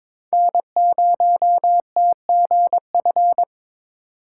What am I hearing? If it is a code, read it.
Morse code, heard as a keyed tone, about 22 wpm, 700 Hz. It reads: N0TGF
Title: USA Call Signs - 5 Characters - 3x Character Spacing - 22wpm